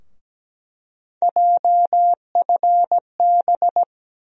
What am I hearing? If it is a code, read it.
Morse code, heard as a keyed tone, about 17 wpm, 700 Hz. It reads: JFB
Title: USA Call Sign Postfixes - 3 Letters 17wpm